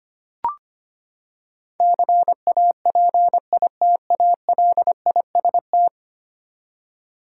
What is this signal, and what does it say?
Morse code, heard as a keyed tone, about 25 wpm, 700 Hz. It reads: CAPITALIST
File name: Ten Letter Words - 25wpm